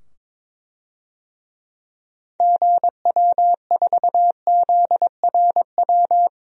Morse code, heard as a keyed tone, about 22 wpm, 700 Hz. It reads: GW4ZRW
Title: DXCC-100 Call Signs - 6 Characters - 22wpm